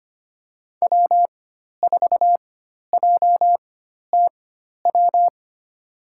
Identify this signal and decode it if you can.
Morse code, heard as a keyed tone, about 25 wpm, spaced as for 12 wpm, 700 Hz. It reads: W4JTW
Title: USA Call Signs - 5 Characters - 4x Character Spacing - 25wpm